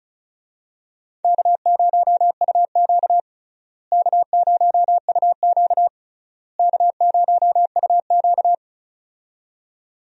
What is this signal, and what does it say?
Morse code, heard as a keyed tone, about 36 wpm, 700 Hz. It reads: K0UQ K0UQ K0UQ
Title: USA Call Signs - 4 Characters - Repeated 3x - 35wpm